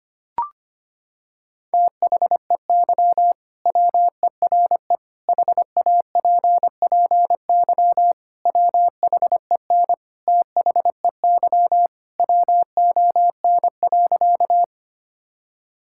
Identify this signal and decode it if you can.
Morse code, heard as a keyed tone, about 25 wpm, 700 Hz. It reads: THEY WERE HAPPY WHEN THEY WON.
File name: practice / 25wpm / Sentences with 2-Word Phrases - 25wpm